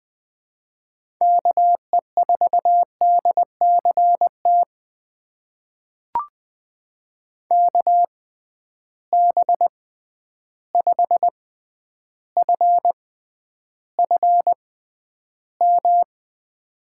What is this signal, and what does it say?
Morse code, heard as a keyed tone, about 21 wpm, spaced as for 10 wpm, 700 Hz. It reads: KE4DCT KB5FFM
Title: USA Call Signs - 6 Characters - 6x Character Spacing - 20wpm